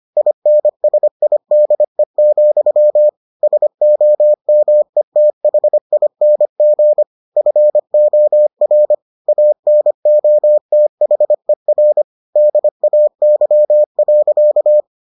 Morse code, heard as a keyed tone, about 25 wpm, 600 Hz. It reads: INSIDE, SOMETHING FOR ANOTHER DAY.